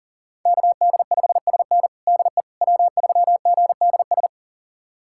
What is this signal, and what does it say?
Morse code, heard as a keyed tone, about 40 wpm, 700 Hz. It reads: KD5SN DE W3GDS
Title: USA Callsign DE Callsign 40wpm